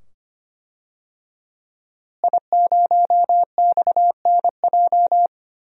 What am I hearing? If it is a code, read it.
Morse code, heard as a keyed tone, about 25 wpm, 700 Hz. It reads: I0XNJ
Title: DXCC-100 Call Signs - 5 Characters - Repeated 2x - 25wpm